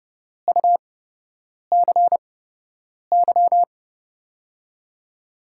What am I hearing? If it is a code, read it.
Morse code, heard as a keyed tone, about 29 wpm, spaced as for 8 wpm, 700 Hz. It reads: UCY
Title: USA Call Signs - 5 Characters - 8x Character Spacing - 30wpm